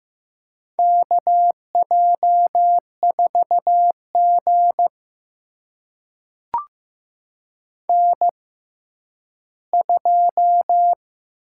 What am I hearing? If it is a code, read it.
Morse code, heard as a keyed tone, about 15 wpm, spaced as for 10 wpm, 700 Hz. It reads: KJ4G N2
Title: USA Call Signs - 4 Characters - 6x Character Spacing - 15wpm